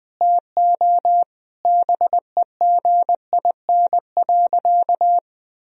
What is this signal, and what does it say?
Morse code, heard as a keyed tone, about 21 wpm, 700 Hz. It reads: TO BEGIN.
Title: Sentences from Top 1000 Words - Review Part 1 - 20wpm